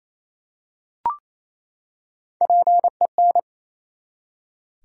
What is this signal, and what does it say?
Morse code, heard as a keyed tone, about 28 wpm, 700 Hz. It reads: PEN
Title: Three Letter Words - 28wpm